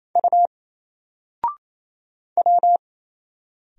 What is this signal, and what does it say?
Morse code, heard as a keyed tone, about 28 wpm, 700 Hz. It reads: U W
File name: Single-Letter-Number-28wpm